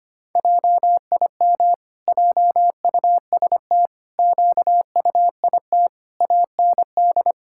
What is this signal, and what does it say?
Morse code, heard as a keyed tone, about 25 wpm, 700 Hz. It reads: JIM JUST QUIT AND
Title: Pangram Challenge v2 - 25wpm